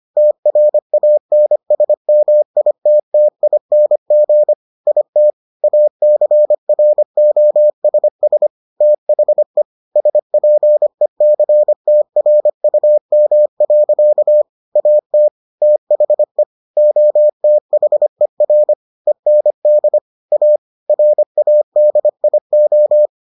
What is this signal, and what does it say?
Morse code, heard as a keyed tone, about 25 wpm, 600 Hz. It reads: TRANSMITTING IT ACROSS THE SPECTRUM. AT THE OTHER END A RADIO